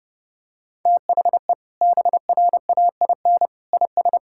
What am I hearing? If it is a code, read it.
Morse code, heard as a keyed tone, about 32 wpm, 700 Hz. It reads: THE BRAIN IS
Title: Sentences from Top 1000 Words - Review Part 6 - 30wpm